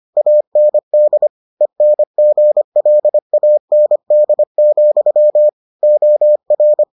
Morse code, heard as a keyed tone, about 25 wpm, 600 Hz. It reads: AND ENGLAND, OR